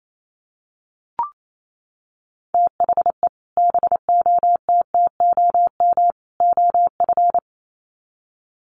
Morse code, heard as a keyed tone, about 28 wpm, 700 Hz. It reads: THE BOTTOM OF
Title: Sets of 3 Words 28wpm